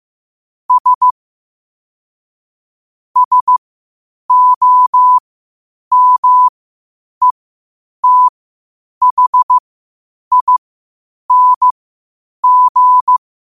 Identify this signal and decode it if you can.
Morse code, heard as a keyed tone, about 15 wpm, spaced as for 8 wpm, 1000 Hz. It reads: S SOMETHING
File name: qotdfiles / 8wpm